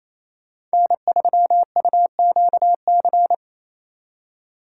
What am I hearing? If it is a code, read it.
Morse code, heard as a keyed tone, about 36 wpm, 700 Hz. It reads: N3UQC